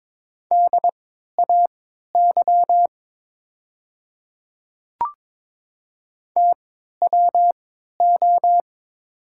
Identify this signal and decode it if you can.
Morse code, heard as a keyed tone, about 22 wpm, spaced as for 13 wpm, 700 Hz. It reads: DAY TWO
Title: 3x Character Spacing - Top 200 Words - 22wpm